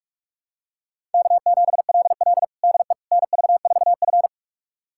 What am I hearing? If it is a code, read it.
Morse code, heard as a keyed tone, about 46 wpm, 700 Hz. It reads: K7LL DE NV4F
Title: USA Callsign DE Callsign - 4 Characters - 45wpm